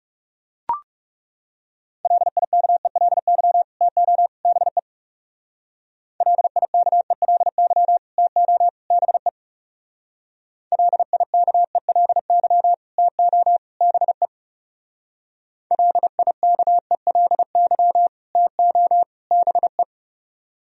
Morse code, heard as a keyed tone, about 37 wpm, 700 Hz. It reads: LIKELY TO BE LIKELY TO BE LIKELY TO BE LIKELY TO BE
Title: Speed-Racing - Sets of 3 Words 30wpm